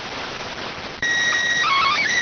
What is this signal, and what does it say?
SSTV (slow-scan television), PD 290.